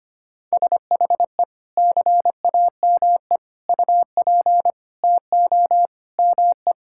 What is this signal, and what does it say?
Morse code, heard as a keyed tone, about 25 wpm, 700 Hz. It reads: SHE CAME UP TO ME
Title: Sentences from Top 1600 Words 25wpm